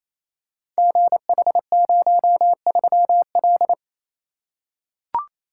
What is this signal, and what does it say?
Morse code, heard as a keyed tone, about 29 wpm, 700 Hz. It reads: GH03L
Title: Code Groups - 5 Letter:Numbers - 28wpm